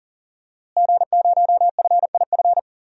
Morse code, heard as a keyed tone, about 40 wpm, 700 Hz. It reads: G0FIF
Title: DXCC-100 Call Signs - 5 Characters - Repeated 3x - 40wpm